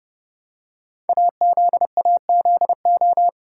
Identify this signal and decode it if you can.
Morse code, heard as a keyed tone, about 30 wpm, 700 Hz. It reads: AZAZO
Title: Code Groups - 5 Letters - 30wpm